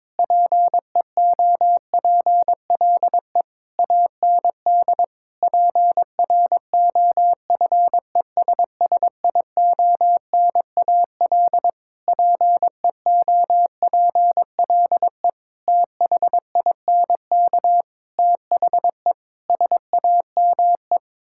Morse code, heard as a keyed tone, about 22 wpm, 700 Hz. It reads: PEOPLE AND PROFESSIONAL PEOPLE THINK THE SAME
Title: Sentences from Top 1000 Words - Review Part 5 - 22wpm